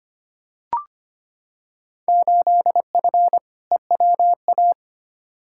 Morse code, heard as a keyed tone, about 25 wpm, 700 Hz. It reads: 8F EWA
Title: Field Day Exchanges Encore - 25wpm